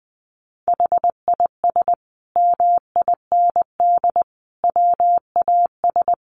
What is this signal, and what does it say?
Morse code, heard as a keyed tone, about 20 wpm, 700 Hz. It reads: HIS MIND WAS
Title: Sentences from Top 1500 Words 20wpm